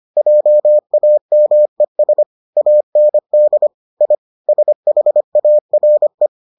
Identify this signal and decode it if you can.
Morse code, heard as a keyed tone, about 25 wpm, 600 Hz. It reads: JAMES AND I SHARE